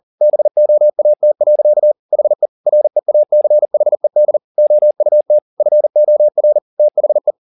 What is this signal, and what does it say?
Morse code, heard as a keyed tone, about 40 wpm, 600 Hz. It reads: BOAT. HE REACHED OUT FOR THE